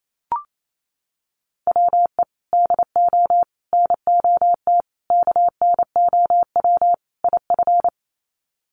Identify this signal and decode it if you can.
Morse code, heard as a keyed tone, about 28 wpm, 700 Hz. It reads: WE DO NOT KNOW IF
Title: Sets of 5 Words Encore2 28wpm